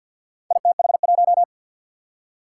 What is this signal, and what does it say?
Morse code, heard as a keyed tone, about 47 wpm, 700 Hz. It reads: ITH1
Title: Four Challenging Letter-Number Combinations 50wpm